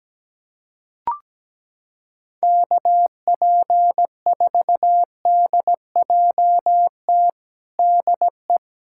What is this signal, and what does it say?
Morse code, heard as a keyed tone, about 17 wpm, 700 Hz. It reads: KP4DJT DE
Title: USA Callsign DE Callsign - 6 Characters - 17wpm